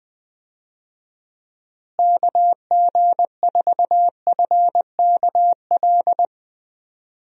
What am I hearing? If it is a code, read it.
Morse code, heard as a keyed tone, about 21 wpm, 700 Hz. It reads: KG4FKL
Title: USA Call Signs - 6 Characters - Repeated 2x - 20wpm